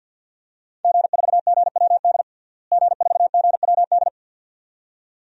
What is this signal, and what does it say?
Morse code, heard as a keyed tone, about 51 wpm, 700 Hz. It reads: G4GWD G4GWD
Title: DXCC-100 Call Signs - 5 Characters - Repeated 2x - 50wpm